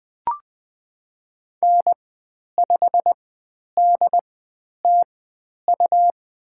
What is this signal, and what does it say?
Morse code, heard as a keyed tone, about 20 wpm, spaced as for 10 wpm, 700 Hz. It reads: N5DTU
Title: 102 Lesson 17g - Letter u - Call Signs 20wpm